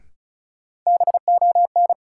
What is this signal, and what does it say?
Morse code, heard as a keyed tone, about 34 wpm, 700 Hz. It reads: BON
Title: Sentences from Top 1600 Words - New Word Drill - 35wpm